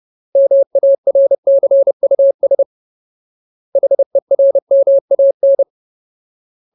Morse code, heard as a keyed tone, about 30 wpm, 550 Hz. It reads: MARCUS HERMAN
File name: Ultimate Word List / 30WPM Ultimate